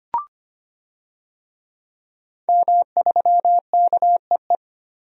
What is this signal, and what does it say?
Morse code, heard as a keyed tone, about 25 wpm, 700 Hz. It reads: M3KEE
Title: DXCC-100 Call Signs - 5 Characters - 25wpm